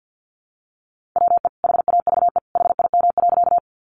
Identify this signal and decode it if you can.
Morse code, heard as a keyed tone, about 50 wpm, 700 Hz. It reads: WE HAVE HIM.
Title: Sentences from Top 100 Words 50wpm